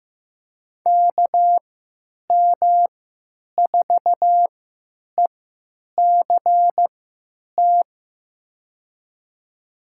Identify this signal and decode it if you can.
Morse code, heard as a keyed tone, about 15 wpm, spaced as for 8 wpm, 700 Hz. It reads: KM4ECT
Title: USA Call Signs - 6 Characters - 3x Character Spacing - 15wpm